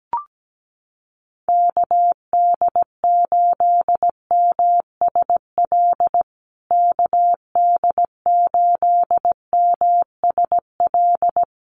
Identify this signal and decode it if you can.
Morse code, heard as a keyed tone, about 17 wpm, 700 Hz. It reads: KD8MSL KD8MSL
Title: USA Call Signs Repeated 3x 17wpm